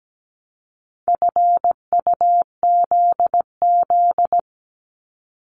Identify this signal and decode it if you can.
Morse code, heard as a keyed tone, about 17 wpm, 700 Hz. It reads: FUZZ
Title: Multi-letter Phonemes 17wpm